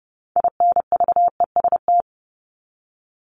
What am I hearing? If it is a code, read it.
Morse code, heard as a keyed tone, about 30 wpm, 700 Hz. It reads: INVEST
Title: Top 1001-1250 Words 30wpm